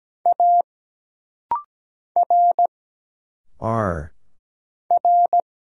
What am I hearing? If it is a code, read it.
Morse code, heard as a keyed tone, about 17 wpm, 700 Hz. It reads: A R R
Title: Single Letters - Rapid-Fire 17wpm